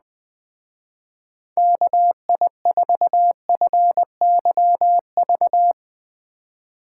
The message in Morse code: KI4FYV

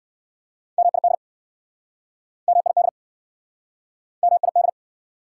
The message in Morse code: MID MID MID

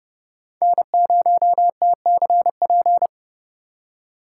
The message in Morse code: N0TCP

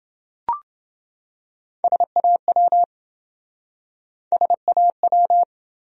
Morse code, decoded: SAW SAW

